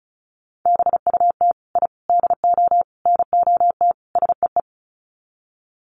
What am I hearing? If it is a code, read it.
Morse code, heard as a keyed tone, about 34 wpm, 700 Hz. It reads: BUT I DO NOT SEE
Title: Sets of 5 Words Encore 35wpm